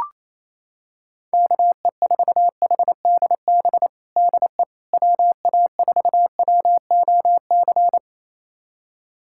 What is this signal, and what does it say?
Morse code, heard as a keyed tone, about 28 wpm, 700 Hz. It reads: KE4HDB DE WA4WOC
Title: USA Callsign DE Callsign - 6 Characters - 28wpm